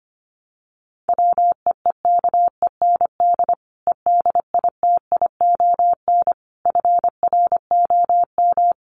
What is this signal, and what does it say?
Morse code, heard as a keyed tone, about 25 wpm, 700 Hz. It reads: WEEKEND EDITION FROM
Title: Sets of 5 Words 25wpm